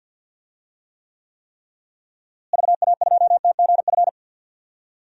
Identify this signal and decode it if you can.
Morse code, heard as a keyed tone, about 50 wpm, 700 Hz. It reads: VA2TGF
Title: DXCC-100 Call Signs - 6 Characters - Repeated 3x - 50wpm